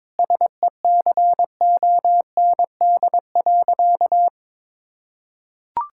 Morse code, heard as a keyed tone, about 21 wpm, 700 Hz. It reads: SECOND.
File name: Difficult Sentences with Hints - 22wpm